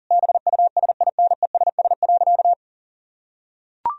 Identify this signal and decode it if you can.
Morse code, heard as a keyed tone, about 40 wpm, 700 Hz. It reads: BUSINESS.
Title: Sentences from Top 1000 Words - Review Part 4 - 40wpm